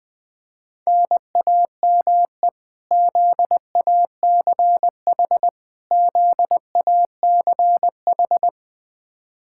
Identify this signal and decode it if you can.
Morse code, heard as a keyed tone, about 20 wpm, 700 Hz. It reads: NAME ZACH ZACH
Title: Practice QSO - Exchange 1 - Beginning - 20wpm